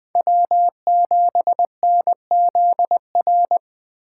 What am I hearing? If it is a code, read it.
Morse code, heard as a keyed tone, about 21 wpm, 700 Hz. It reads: W7NZR
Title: USA Call Signs - 5 Characters - 3x Character Spacing - 20wpm